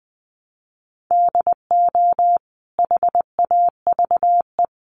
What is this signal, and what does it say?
Morse code, heard as a keyed tone, about 20 wpm, 700 Hz. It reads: DO HAVE
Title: Sentences from Top 1300 Words 20wpm